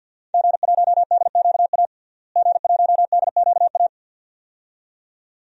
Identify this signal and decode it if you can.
Morse code, heard as a keyed tone, about 47 wpm, 700 Hz. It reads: G1DQA G1DQA
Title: DXCC-100 Call Signs - 5 Characters - Repeated 2x - 50wpm